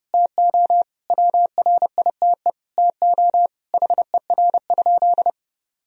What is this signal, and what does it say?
Morse code, heard as a keyed tone, about 32 wpm, 700 Hz. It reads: TO WRITE TO HER?
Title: Sentences from Top 1000 Words - Review Part 1 - 30wpm